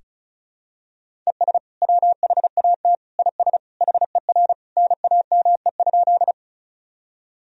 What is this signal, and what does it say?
Morse code, heard as a keyed tone, about 36 wpm, 690 Hz. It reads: ES WHAT IS HER NAME?
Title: Sentences from Top 200 Words and Abbreviations 35wpm